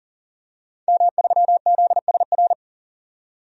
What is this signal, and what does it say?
Morse code, heard as a keyed tone, about 40 wpm, 700 Hz. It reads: M3ZSR
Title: DXCC-100 Call Signs - 5 Characters - 40wpm